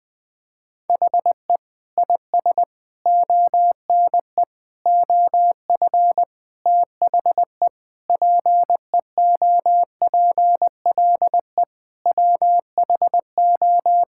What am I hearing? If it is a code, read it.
Morse code, heard as a keyed tone, about 20 wpm, 700 Hz. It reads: HE IS ONE OF THE PEOPLE WHO
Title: Sentences with 4-Word Phrases - 20wpm